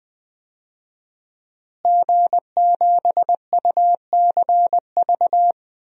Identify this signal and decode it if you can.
Morse code, heard as a keyed tone, about 20 wpm, 700 Hz. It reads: G7UCV
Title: DXCC-100 Call Signs - 5 Characters - Repeated 2x - 20wpm